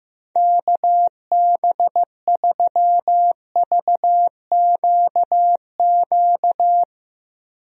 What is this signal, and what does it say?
Morse code, heard as a keyed tone, about 15 wpm, 700 Hz. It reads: KB3VQQ
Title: USA Call Signs - 6 Characters - 15wpm